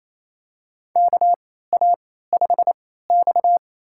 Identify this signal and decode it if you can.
Morse code, heard as a keyed tone, about 28 wpm, spaced as for 16 wpm, 690 Hz. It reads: KA5X
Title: USA Call Signs - 3x Character Spacing - 28wpm